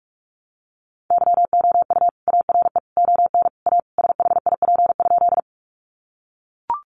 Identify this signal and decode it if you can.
Morse code, heard as a keyed tone, about 44 wpm, 700 Hz. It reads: YOU ARE ON A SHIP?
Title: Sentences from Top 400 Words 45wpm